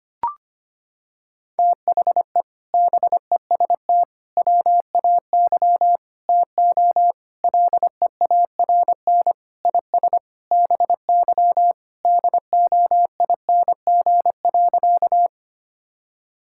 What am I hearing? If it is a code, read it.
Morse code, heard as a keyed tone, about 25 wpm, 700 Hz. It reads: THE BEST WAY TO LEARN IS BY DOING.
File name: practice / 25wpm / Sentences with 4-Word Phrases - 25wpm